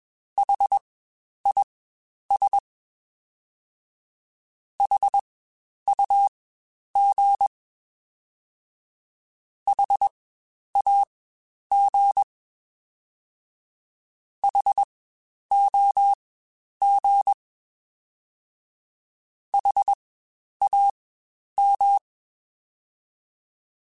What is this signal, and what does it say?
Morse code, heard as a keyed tone, about 21 wpm, spaced as for 10 wpm, 800 Hz. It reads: HIS HUG HAG HOG HAM
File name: Spaced Alphas / Spaced Alphas mp3s / Hs sp @20